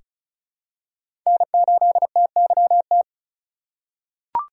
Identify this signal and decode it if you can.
Morse code, heard as a keyed tone, about 36 wpm, 700 Hz. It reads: N8TYT
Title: USA Call Signs - 5 Characters - 2x Character Spacing - 35wpm